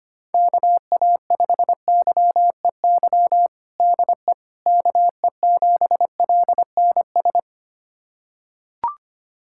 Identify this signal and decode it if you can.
Morse code, heard as a keyed tone, about 25 wpm, 700 Hz. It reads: KA5YEY DE KE7LNS
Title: USA Callsign DE Callsign 25wpm